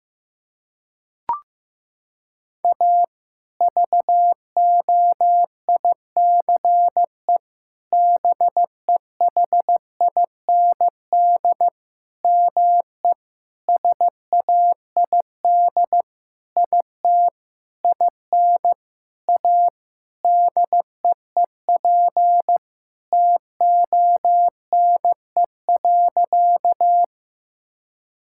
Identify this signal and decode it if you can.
Morse code, heard as a keyed tone, about 15 wpm, 700 Hz. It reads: A VOICE BEHIND ME SAID IT IN A DEEP TONE.
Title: Sentences from Top 1600 Words 15wpm